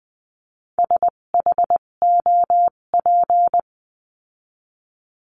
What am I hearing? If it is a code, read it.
Morse code, heard as a keyed tone, about 20 wpm, spaced as for 17 wpm, 700 Hz. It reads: SHOP
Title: 257 Lesson 42b - Words 14-20wpm